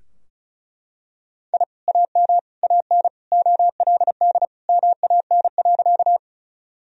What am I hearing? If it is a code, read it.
Morse code, heard as a keyed tone, about 36 wpm, 700 Hz. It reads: I AM AN OLD MAN.
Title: Sentences from Top 1000 Words - Review Part 2 - 35wpm